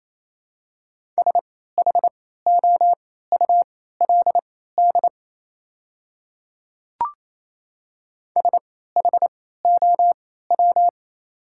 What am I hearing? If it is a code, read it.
Morse code, heard as a keyed tone, about 28 wpm, spaced as for 16 wpm, 700 Hz. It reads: SHOULD SHOW
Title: 3x Character Spacing - Top 300 Words - 28wpm